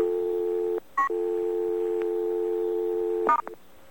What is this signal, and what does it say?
Dial tone (North America), 350 + 440 Hz.